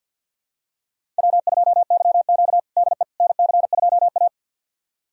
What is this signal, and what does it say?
Morse code, heard as a keyed tone, about 50 wpm, 700 Hz. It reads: W2YQ DE NC2A
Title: USA Callsign DE Callsign - 4 Characters - 50wpm